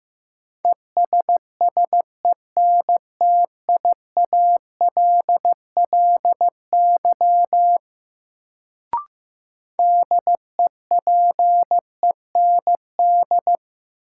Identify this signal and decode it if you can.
Morse code, heard as a keyed tone, about 15 wpm, 700 Hz. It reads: ESSENTIALLY DEPEND
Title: Sentences from Top 1300 Words - New Word Drill Alt - 15wpm